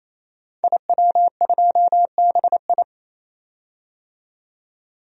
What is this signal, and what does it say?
Morse code, heard as a keyed tone, about 28 wpm, 700 Hz. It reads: IW2BI
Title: DXCC-100 Call Signs - 5 Characters - 28wpm